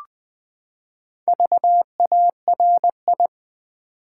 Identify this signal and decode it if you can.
Morse code, heard as a keyed tone, about 20 wpm, 700 Hz. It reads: VARI
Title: Build A Word - 4 Letters At A Time - 8-Letter Words - 20wpm